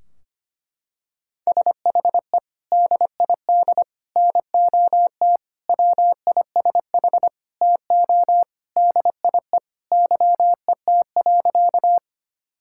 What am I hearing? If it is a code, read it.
Morse code, heard as a keyed tone, about 25 wpm, 700 Hz. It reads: SHE DID NOT WISH TO DIE YET.
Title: Sentences from Top 1600 Words 25wpm